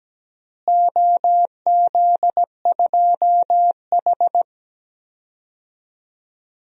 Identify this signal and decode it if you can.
Morse code, heard as a keyed tone, about 17 wpm, 700 Hz. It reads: OZ2H
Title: DXCC-100 Call Signs - 4 Characters - Repeated 3x - 17wpm